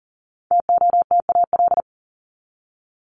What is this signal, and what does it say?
Morse code, heard as a keyed tone, about 40 wpm, 700 Hz. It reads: TOTAL
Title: Top 1251-1500 Words 40wpm